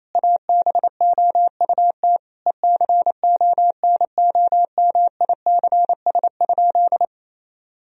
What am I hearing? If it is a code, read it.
Morse code, heard as a keyed tone, about 28 wpm, 700 Hz. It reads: ABOUT ECONOMICS?
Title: Sentences from Top 1000 Words - Review Part 3 - 28wpm